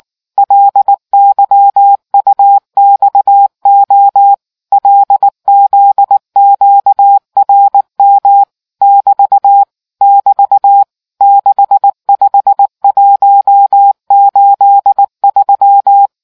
Morse code, heard as a keyed tone, about 19 wpm, 800 Hz. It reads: LYUXO LZQRM <BT> <BT> 65183